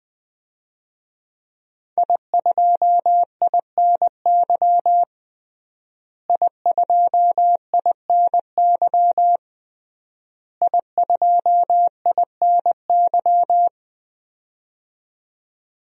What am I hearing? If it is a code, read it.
Morse code, heard as a keyed tone, about 21 wpm, 700 Hz. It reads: I2INY I2INY I2INY
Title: DXCC-100 Call Signs - 5 Characters - Repeated 3x - 20wpm